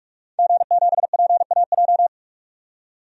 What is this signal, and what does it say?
Morse code, heard as a keyed tone, about 43 wpm, 700 Hz. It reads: G7PAJ
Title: DXCC-100 Call Signs - 5 Characters - 45wpm